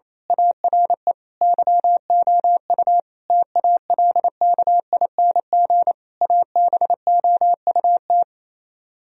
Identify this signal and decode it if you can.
Morse code, heard as a keyed tone, about 27 wpm, 700 Hz. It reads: ARE YOU TALKING ABOUT
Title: Speed-Racing - Sets of 5 Words 28wpm